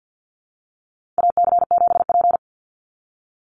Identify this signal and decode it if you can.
Morse code, heard as a keyed tone, about 48 wpm, 700 Hz. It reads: AC7P
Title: NA SOTA Chaser Callsigns 50wpm